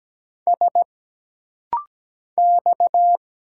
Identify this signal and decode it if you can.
Morse code, heard as a keyed tone, about 17 wpm, 700 Hz. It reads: S X
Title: Single Letters - Rapid-Fire 17wpm